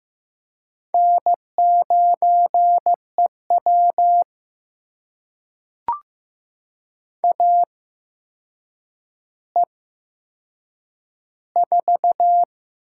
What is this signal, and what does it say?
Morse code, heard as a keyed tone, about 15 wpm, spaced as for 7 wpm, 700 Hz. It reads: N9EW AE4
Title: USA Call Signs - 8x Character Spacing - 15wpm